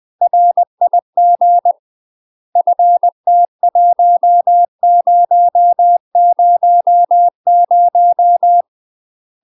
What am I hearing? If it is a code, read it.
Morse code, heard as a keyed tone, about 20 wpm, 700 Hz. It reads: RIG FT1000
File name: UK MASTER QSOs TRAINING / 20WPM / UK MASTER QSOs 0059